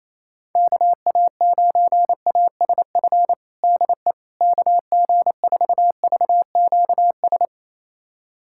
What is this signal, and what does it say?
Morse code, heard as a keyed tone, about 28 wpm, 700 Hz. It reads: KA9ASF DE KG4VQS
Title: USA Callsign DE Callsign 28wpm